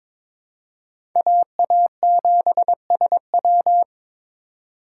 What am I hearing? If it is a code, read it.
Morse code, heard as a keyed tone, about 22 wpm, 700 Hz. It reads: AA7SW
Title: USA Call Signs - 5 Characters - 3x Character Spacing - 22wpm